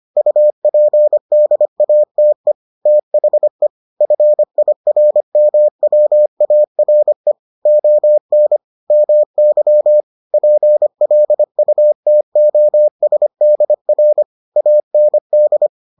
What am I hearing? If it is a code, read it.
Morse code, heard as a keyed tone, about 25 wpm, 600 Hz. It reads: UPDATE THE FIRMWARE ON MY PLUTOSDR AND